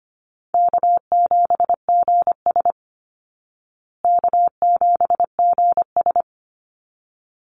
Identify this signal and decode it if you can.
Morse code, heard as a keyed tone, about 25 wpm, 700 Hz. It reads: K7GS K7GS
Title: Call Signs Repeated 3x - 25wpm